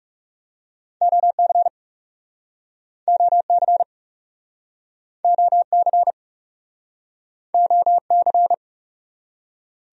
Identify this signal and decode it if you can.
Morse code, heard as a keyed tone, about 38 wpm, 700 Hz. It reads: OC OC OC OC